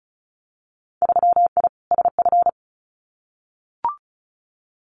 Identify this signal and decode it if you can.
Morse code, heard as a keyed tone, about 34 wpm, 700 Hz. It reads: 3I SF
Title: Winter Field Day Exchanges 35wpm